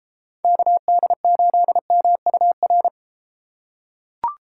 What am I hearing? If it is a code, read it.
Morse code, heard as a keyed tone, about 32 wpm, 700 Hz. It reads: KD8MUR